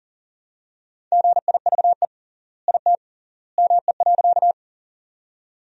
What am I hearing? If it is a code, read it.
Morse code, heard as a keyed tone, about 40 wpm, 700 Hz. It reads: GIVE IT ME.
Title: 3x Word-Spacing - Sentences from Top 200 Words - 40wpm